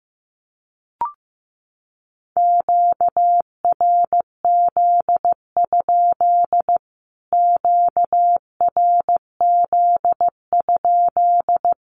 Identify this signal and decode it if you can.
Morse code, heard as a keyed tone, about 15 wpm, 700 Hz. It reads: QRZ? QRZ?
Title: Q-Codes Repeated 3x 15wpm